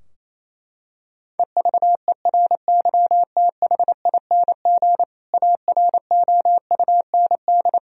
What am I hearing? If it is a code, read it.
Morse code, heard as a keyed tone, about 28 wpm, 700 Hz. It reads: EVERYTHING AROUND
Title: Sentences from Top 1000 Words - Review Part 3 - 28wpm